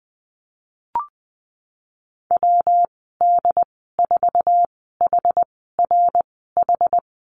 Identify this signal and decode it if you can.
Morse code, heard as a keyed tone, about 20 wpm, spaced as for 15 wpm, 700 Hz. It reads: WD4HRH
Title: 167 Lesson 27g - Number 7 - Call Signs 20wpm